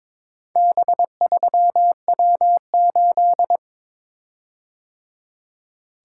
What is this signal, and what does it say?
Morse code, heard as a keyed tone, about 23 wpm, 700 Hz. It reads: B3W8
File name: practice / 22wpm / Four Letter-Number Combinations - 22wpm